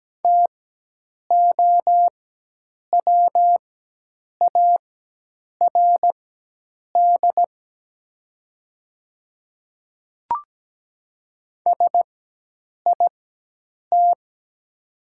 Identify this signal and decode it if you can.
Morse code, heard as a keyed tone, about 17 wpm, spaced as for 8 wpm, 700 Hz. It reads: TOWARD SIT
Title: 4x Character Spacing - Top 500 Words - 17wpm